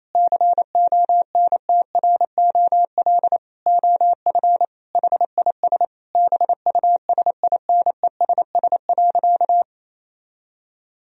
Morse code, heard as a keyed tone, about 28 wpm, 700 Hz. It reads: CONTROL OF HIS BUSINESS.
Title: Sentences from Top 1000 Words - Review Part 3 - 28wpm